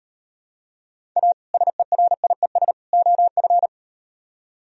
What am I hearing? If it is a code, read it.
Morse code, heard as a keyed tone, about 38 wpm, 700 Hz. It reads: A SERIES OF